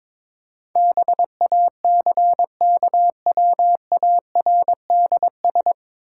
Morse code, heard as a keyed tone, about 22 wpm, 700 Hz. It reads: BACKWARDS
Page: Morse Code Ninja practice 22wpm